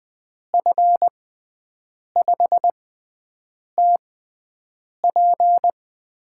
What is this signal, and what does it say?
Morse code, heard as a keyed tone, about 20 wpm, spaced as for 7 wpm, 700 Hz. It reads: F5TP
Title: USA Call Signs - 5 Characters - 6x Character Spacing - 20wpm